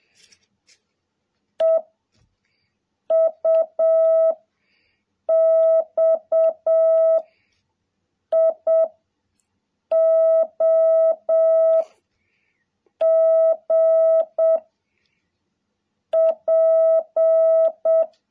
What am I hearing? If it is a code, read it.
Morse code, heard as a keyed tone, about 7 wpm, 650 Hz. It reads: E U X I O G P
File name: jh8cbh / 2021cw